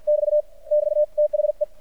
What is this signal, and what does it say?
Morse code, heard as a keyed tone, about 53 wpm, 600 Hz. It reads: <BT> <BT> THE
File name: archived / CRSnet